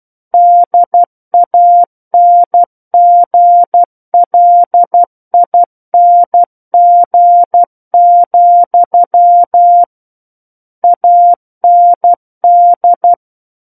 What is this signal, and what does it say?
Morse code, heard as a keyed tone, about 12 wpm, 700 Hz. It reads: DANGLING, AND
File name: Book One / 03-Chapter 3 - 12 WPM